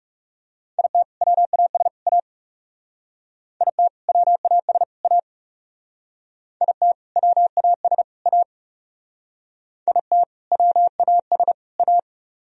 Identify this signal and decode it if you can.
Morse code, heard as a keyed tone, about 37 wpm, 700 Hz. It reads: IT WAS A IT WAS A IT WAS A IT WAS A
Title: Speed-Racing - Sets of 3 Words 30wpm